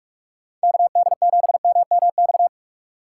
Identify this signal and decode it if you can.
Morse code, heard as a keyed tone, about 48 wpm, 700 Hz. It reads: KD7MMX
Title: USA Call Signs - 6 Characters - 6x Character Spacing - 45wpm